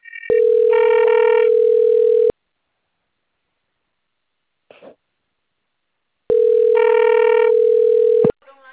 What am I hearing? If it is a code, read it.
Ringback tone (North America), 440 + 480 Hz, with speech.
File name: rg-1600-unknown- - - .10017